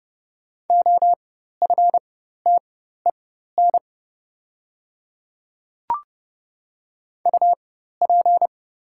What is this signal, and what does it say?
Morse code, heard as a keyed tone, about 30 wpm, spaced as for 14 wpm, 690 Hz. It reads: OFTEN UP
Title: 4x Character Spacing - Top 300 Words - 30wpm